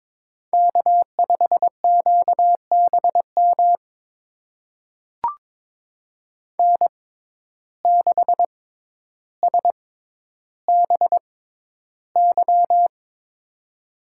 Morse code, heard as a keyed tone, about 22 wpm, spaced as for 11 wpm, 700 Hz. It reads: K5QBM N6SBY
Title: USA Call Signs - 5 Characters - 6x Character Spacing - 22wpm